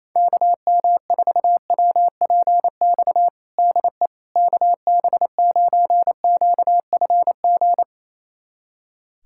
Morse code, heard as a keyed tone, about 28 wpm, 700 Hz. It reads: KM4WPX DE KB9QFG
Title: USA Callsign DE Callsign - 6 Characters - 28wpm